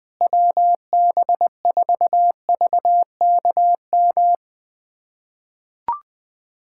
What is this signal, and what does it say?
Morse code, heard as a keyed tone, about 20 wpm, 700 Hz. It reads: WB4VKM
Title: USA Callsign DE Callsign - 6 Characters - 20wpm